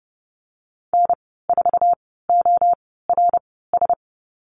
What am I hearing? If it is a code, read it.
Morse code, heard as a keyed tone, about 30 wpm, spaced as for 17 wpm, 700 Hz. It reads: N4ORS
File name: Lesson 14g - Number 2 - Call Signs 30wpm